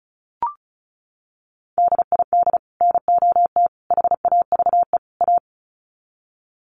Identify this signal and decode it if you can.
Morse code, heard as a keyed tone, about 34 wpm, 700 Hz. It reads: DID NOT HAVE A
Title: Sets of 4 Words 35wpm